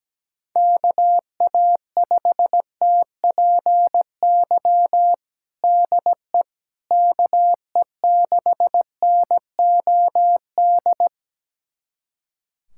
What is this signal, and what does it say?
Morse code, heard as a keyed tone, about 17 wpm, 700 Hz. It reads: KA5TPY DE KE6NOD
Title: USA Callsign DE Callsign - 6 Characters - 17wpm